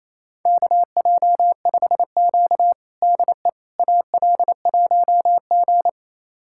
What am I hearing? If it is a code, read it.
Morse code, heard as a keyed tone, about 27 wpm, 700 Hz. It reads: KJ5Q DE AL1G
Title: USA Callsign DE Callsign - 4 Characters - 28wpm